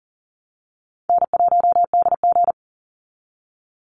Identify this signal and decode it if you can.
Morse code, heard as a keyed tone, about 40 wpm, 700 Hz. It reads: N1DG
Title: Call Signs Repeated 2x - 40wpm